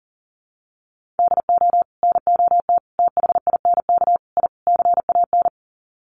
Morse code, heard as a keyed tone, about 40 wpm, 700 Hz. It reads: DO NOT THINK I CAN
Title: Sets of 5 Words Encore2 40wpm